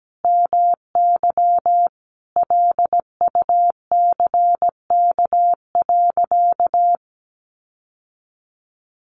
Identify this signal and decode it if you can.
Morse code, heard as a keyed tone, about 17 wpm, 700 Hz. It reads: MY LUCK.
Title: Sentences from Top 1500 Words 17wpm